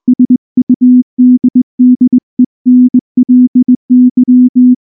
Morse code, heard as a keyed tone, about 18 wpm, 250 Hz. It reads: SUDDENLY
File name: morse / morse-training-20WPM-2